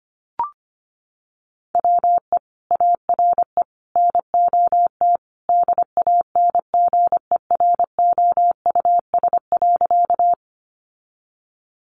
Morse code, heard as a keyed tone, about 25 wpm, 700 Hz. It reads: WE ARE NOT DANGEROUS.
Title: Sentences from Top 1200 Words 25wpm